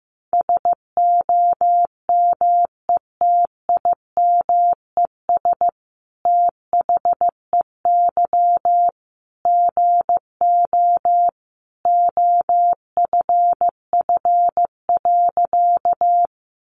Morse code, heard as a keyed tone, about 15 wpm, 700 Hz. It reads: SOMETIMES THEY GO OFF.
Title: Sentences from Top 600 Words 15wpm